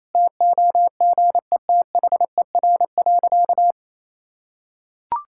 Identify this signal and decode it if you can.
Morse code, heard as a keyed tone, about 28 wpm, 700 Hz. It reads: TOGETHER.
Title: Sentences from Top 1000 Words - Review Part 5 - 28wpm